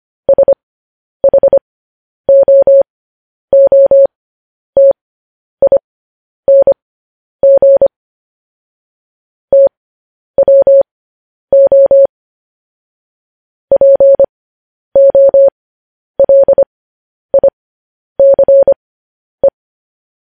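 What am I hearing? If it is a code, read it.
Morse code, heard as a keyed tone, about 25 wpm, spaced as for 10 wpm, 550 Hz. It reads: SHOOTING TWO POLICE
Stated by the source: Morse code news headlines.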